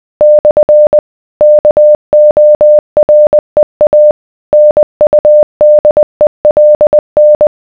Morse code, heard as a keyed tone, about 20 wpm, 600 Hz. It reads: / KOREA NUDELN